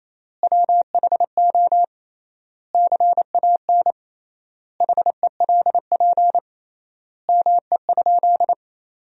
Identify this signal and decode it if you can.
Morse code, heard as a keyed tone, about 27 wpm, 700 Hz. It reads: WHO CAN HELP ME?
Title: 3x Word-Spacing - Sentences from Top 200 Words - 28wpm